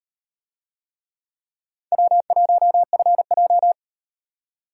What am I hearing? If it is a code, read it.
Morse code, heard as a keyed tone, about 37 wpm, 700 Hz. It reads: W1FJ